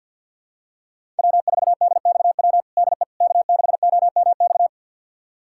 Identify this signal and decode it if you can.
Morse code, heard as a keyed tone, about 51 wpm, 700 Hz. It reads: W4DXW DE K6OMX
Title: USA Callsign DE Callsign - 5 Characters - 50wpm